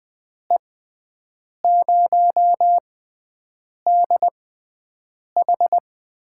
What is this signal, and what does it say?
Morse code, heard as a keyed tone, about 20 wpm, spaced as for 7 wpm, 700 Hz. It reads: E0DH
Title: USA Call Signs - 6 Characters - 6x Character Spacing - 20wpm